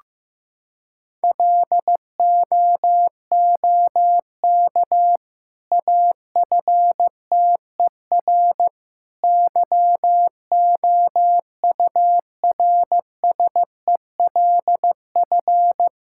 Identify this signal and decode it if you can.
Morse code, heard as a keyed tone, about 15 wpm, 700 Hz. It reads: LOOK AFTER YOURSELF
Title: Sentences from Top 1000 Words - Review Part 6 - 15wpm